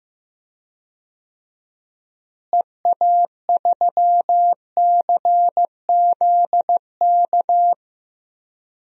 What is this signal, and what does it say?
Morse code, heard as a keyed tone, about 15 wpm, 700 Hz. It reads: EA3CZK
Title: DXCC-100 Call Signs - 6 Characters - Repeated 2x - 15wpm